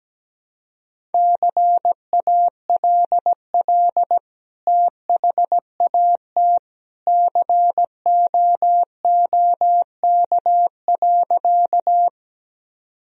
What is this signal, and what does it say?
Morse code, heard as a keyed tone, about 17 wpm, 700 Hz. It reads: CALL THAT COOK.
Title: Sentences from Top 1000 Words - Review Part 4 - 17wpm